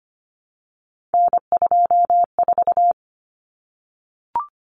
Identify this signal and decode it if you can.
Morse code, heard as a keyed tone, about 25 wpm, 700 Hz. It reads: N24
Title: Common Cut Numbers - 3-Digit 25wpm